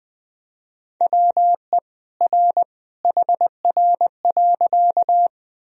Morse code, heard as a keyed tone, about 20 wpm, 700 Hz. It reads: WE R HR.
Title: Sentences from Top 200 Words and Abbreviations 20wpm